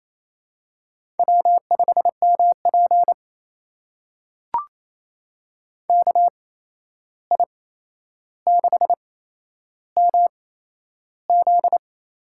Morse code, heard as a keyed tone, about 27 wpm, spaced as for 11 wpm, 700 Hz. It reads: W5MP KI6MZ
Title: USA Call Signs - 8x Character Spacing - 28wpm